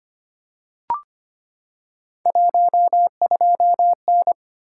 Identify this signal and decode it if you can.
Morse code, heard as a keyed tone, about 25 wpm, 700 Hz. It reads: 12N
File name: Three Letter-Number Combinations - 25wpm